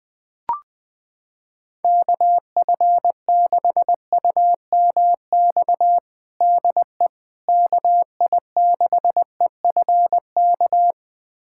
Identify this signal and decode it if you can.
Morse code, heard as a keyed tone, about 20 wpm, 700 Hz. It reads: KF6UMX DE KI6EFK
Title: USA Callsign DE Callsign - 6 Characters - 20wpm